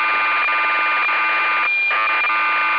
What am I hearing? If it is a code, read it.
Cassette data (Kansas City type).